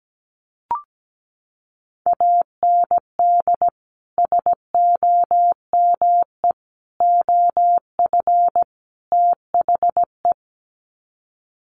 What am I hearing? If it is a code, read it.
Morse code, heard as a keyed tone, about 17 wpm, 700 Hz. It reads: AND SOME OF THE
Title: Sets of 4 Words 17wpm